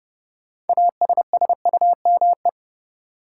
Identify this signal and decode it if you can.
Morse code, heard as a keyed tone, about 31 wpm, 700 Hz. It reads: ASSUME
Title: Sentences from Top 1500 Words - New Word Drill Alt - 30wpm